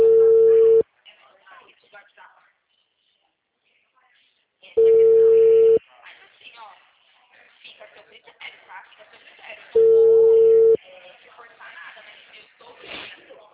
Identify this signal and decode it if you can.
Ringback tone (China, 450 Hz), with speech.
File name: out- -2021- - - .9147